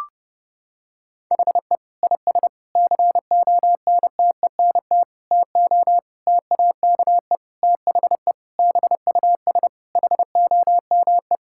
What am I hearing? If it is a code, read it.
Morse code, heard as a keyed tone, about 32 wpm, 700 Hz. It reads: HE IS CONTENT TO TAKE THE BUS HOME
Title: Sentences from Top 1000 Words - Review Part 6 - 30wpm